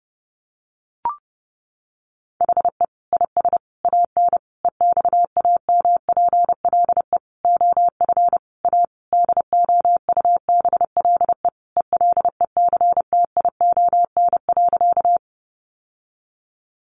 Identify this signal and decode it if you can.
Morse code, heard as a keyed tone, about 30 wpm, 700 Hz. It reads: HE IS AN EXAMPLE OF A DOUBLE ELECTION.
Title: Sentences from Top 1500 Words 30wpm